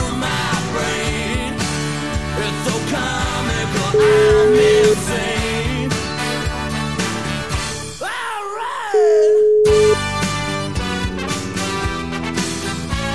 Ringback tone (Europe, 425 Hz), with noise.